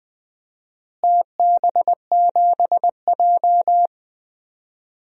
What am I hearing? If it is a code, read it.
Morse code, heard as a keyed tone, about 20 wpm, 700 Hz. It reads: TB7J
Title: DXCC-100 Call Signs - 4 Characters - Repeated 2x - 20wpm